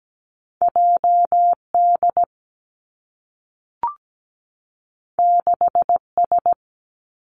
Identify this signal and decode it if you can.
Morse code, heard as a keyed tone, about 17 wpm, 700 Hz. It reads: J D 6 S
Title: Two Challenging Letter-Number Combinations 17wpm